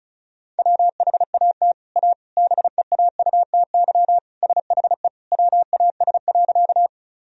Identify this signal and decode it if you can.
Morse code, heard as a keyed tone, about 34 wpm, 700 Hz. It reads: WHAT A BEAUTY SHE WAS.
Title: Sentences from Top 1000 Words - Review Part 2 - 35wpm